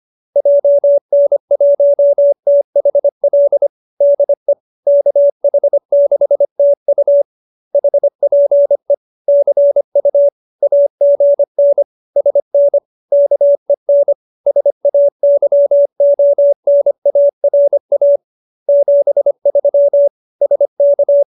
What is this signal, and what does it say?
Morse code, heard as a keyed tone, about 25 wpm, 570 Hz. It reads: JN1THL DE KH6TU HPE CU AGN SN KEN SAYONARA 73 SK